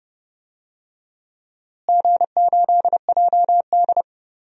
Morse code, heard as a keyed tone, about 30 wpm, 700 Hz. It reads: G8JD
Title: DXCC-100 Call Signs - 4 Characters - Repeated 2x - 30wpm